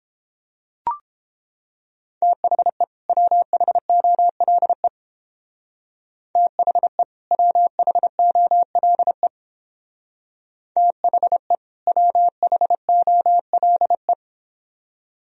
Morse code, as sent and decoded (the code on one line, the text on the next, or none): - .... / .-- .... --- .-.. / - .... / .-- .... --- .-.. / - .... / .-- .... --- .-.. .
THE WHOLE THE WHOLE THE WHOLE